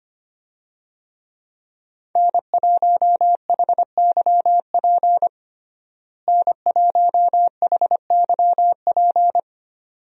Morse code, heard as a keyed tone, about 25 wpm, 690 Hz. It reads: N1HYP N1HYP
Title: USA Call Signs - 5 Characters - Repeated 2x - 25wpm